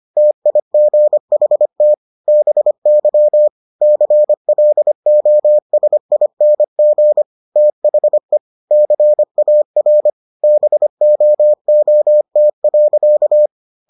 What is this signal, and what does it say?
Morse code, heard as a keyed tone, about 25 wpm, 600 Hz. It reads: TIGHT BY CLOSING THE CAR BOOT.